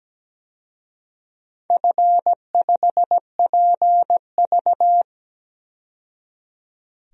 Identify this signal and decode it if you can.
Morse code, heard as a keyed tone, about 17 wpm, 700 Hz. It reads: F5PV
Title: DXCC-100 Call Signs - 4 Characters - 17wpm